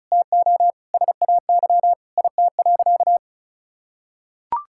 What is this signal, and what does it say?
Morse code, heard as a keyed tone, about 34 wpm, 700 Hz. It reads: TO SAY IT.
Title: Sentences from Top 1600 Words 35wpm